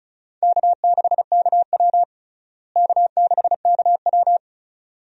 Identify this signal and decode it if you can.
Morse code, heard as a keyed tone, about 34 wpm, 700 Hz. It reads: K6KW K6KW
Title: USA Call Signs - 4 Characters - Repeated 2x - 35wpm